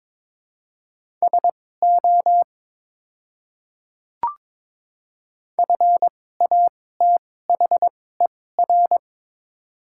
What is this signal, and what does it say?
Morse code, heard as a keyed tone, about 22 wpm, spaced as for 16 wpm, 700 Hz. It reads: SO FATHER
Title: 2x Character Spacing - Top 200 Words - 22wpm